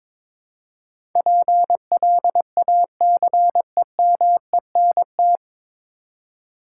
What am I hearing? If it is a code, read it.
Morse code, heard as a keyed tone, about 22 wpm, 700 Hz. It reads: PLACEMENT
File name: practice / 22wpm / 9 Letter Words - First 3 Letters Given - 22wpm